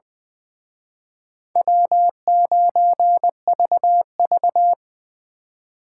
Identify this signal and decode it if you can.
Morse code, heard as a keyed tone, about 24 wpm, 700 Hz. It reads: W9VV
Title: Speed Racing - USA Call Signs 20wpm